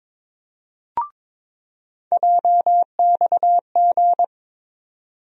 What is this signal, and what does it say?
Morse code, heard as a keyed tone, about 22 wpm, 700 Hz. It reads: JXG
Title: USA Call Sign Postfixes - 3 Letters 22wpm